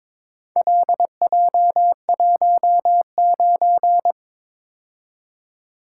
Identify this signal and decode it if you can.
Morse code, heard as a keyed tone, about 22 wpm, 700 Hz. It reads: LJ19
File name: practice / 22wpm / Four Letter-Number Combinations - 22wpm